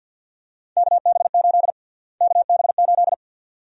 Morse code, heard as a keyed tone, about 47 wpm, 700 Hz. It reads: KB8 KB8
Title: USA Call Sign Prefixes - Repeated 2x - 50wpm